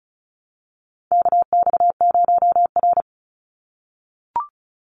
Morse code, heard as a keyed tone, about 34 wpm, 700 Hz. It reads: KX0R
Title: NA SOTA Activator Callsigns 35wpm